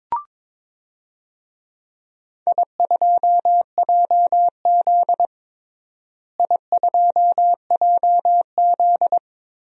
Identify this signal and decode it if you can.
Morse code, heard as a keyed tone, about 22 wpm, 700 Hz. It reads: I2JZ I2JZ
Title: DXCC-100 Call Signs - 4 Characters - Repeated 3x - 22wpm